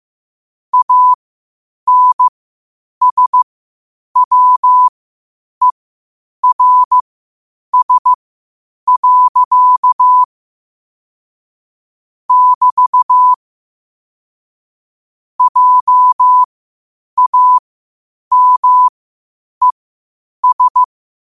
Morse code, heard as a keyed tone, about 15 wpm, spaced as for 8 wpm, 1000 Hz. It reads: ANSWERS. <BT> JAMES
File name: qotdfiles / 8wpm